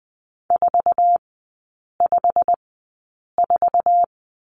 Morse code, heard as a keyed tone, about 20 wpm, spaced as for 8 wpm, 700 Hz. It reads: 454
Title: 107 Lesson 18e - Letter m - Three Characters 20wpm